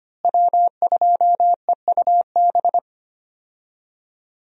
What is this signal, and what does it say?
Morse code, heard as a keyed tone, about 25 wpm, 700 Hz. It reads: W2EUB
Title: USA Call Signs - 5 Characters - Repeated 3x - 25wpm